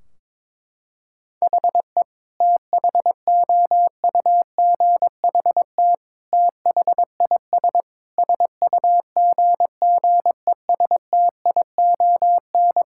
Morse code, heard as a keyed tone, about 22 wpm, 700 Hz. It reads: HE THOUGHT THIS SUGGESTION
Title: Sentences from Top 1600 Words 22wpm